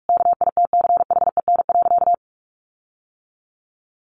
Morse code, heard as a keyed tone, about 44 wpm, 700 Hz. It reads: KITCHEN.
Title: Sentences from Top 1400 Words 45wpm